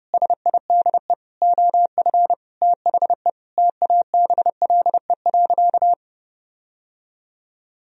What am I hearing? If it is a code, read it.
Morse code, heard as a keyed tone, about 30 wpm, 700 Hz. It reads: SIDE OF THE TABLE.
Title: Sentences from Top 1000 Words - Review Part 4 - 30wpm